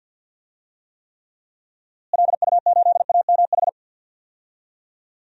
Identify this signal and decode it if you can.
Morse code, heard as a keyed tone, about 49 wpm, 700 Hz. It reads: LU8AMH
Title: DXCC-100 Call Signs - 6 Characters - Repeated 2x - 50wpm